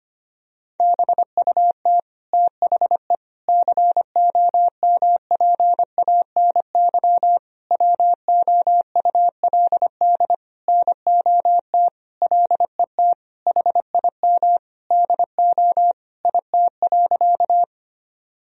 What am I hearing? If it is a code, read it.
Morse code, heard as a keyed tone, about 25 wpm, 700 Hz. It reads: BUT THE COMPANY WOULD NOT LET HIM DO IT.
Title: Sentences from Top 1000 Words - Review Part 3 - 25wpm